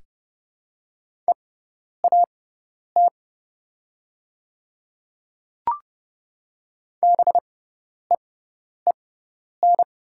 Morse code, heard as a keyed tone, about 30 wpm, spaced as for 10 wpm, 700 Hz. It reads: EAT BEEN
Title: 6x Character Spacing - Top 500 Words - 30wpm